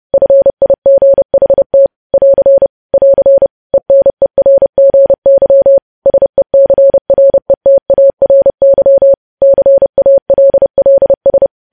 Morse code, heard as a keyed tone, about 30 wpm, 550 Hz. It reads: FIGHT <AR> <AR> ENERGY SECRETARY CALLS
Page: news headlines in morse code at 30 wpm